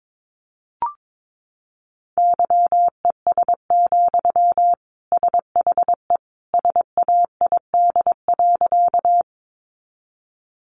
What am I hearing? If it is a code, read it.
Morse code, heard as a keyed tone, about 22 wpm, 700 Hz. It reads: YES, SHE SAID.
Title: Sentences from Top 500 Words 22wpm